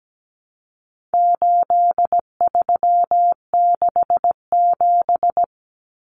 Morse code, heard as a keyed tone, about 17 wpm, 700 Hz. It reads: 8367
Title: Common Cut Numbers - 4-Digit 17wpm